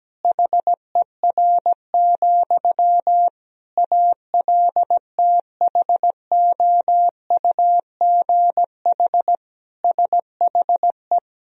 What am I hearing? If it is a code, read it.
Morse code, heard as a keyed tone, about 17 wpm, 700 Hz. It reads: HER, ALTHOUGH SHE
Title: Sentences from Top 1000 Words - Review Part 3 - 17wpm